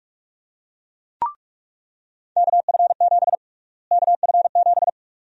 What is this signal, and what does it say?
Morse code, heard as a keyed tone, about 46 wpm, 700 Hz. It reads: KF7 KF7
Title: USA Call Sign Prefixes - Repeated 2x - 45wpm